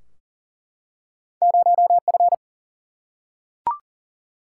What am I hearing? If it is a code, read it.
Morse code, heard as a keyed tone, about 41 wpm, 700 Hz. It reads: 0F
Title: Single-Letter Single-Number 40wpm